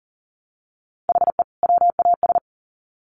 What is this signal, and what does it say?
Morse code, heard as a keyed tone, about 40 wpm, 700 Hz. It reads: HE WAS
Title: Sets of 2 Words 40wpm